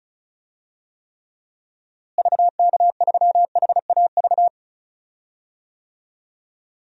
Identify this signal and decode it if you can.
Morse code, heard as a keyed tone, about 34 wpm, 700 Hz. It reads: VK3HAV
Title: DXCC-100 Call Signs - 6 Characters - Repeated 2x - 35wpm